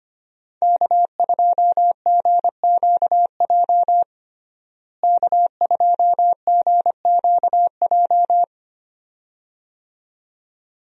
Morse code, heard as a keyed tone, about 25 wpm, 700 Hz. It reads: K2GQJ K2GQJ
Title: USA Call Signs - 5 Characters - Repeated 2x - 25wpm